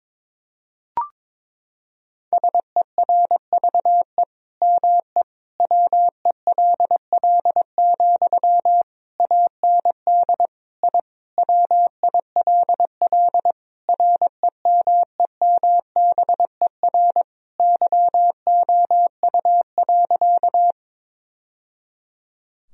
Morse code, heard as a keyed tone, about 22 wpm, 700 Hz. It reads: SERVE ME WELL, AND I WILL REMEMBER YOU.
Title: Sentences from Top 1000 Words - Review Part 2 - 22wpm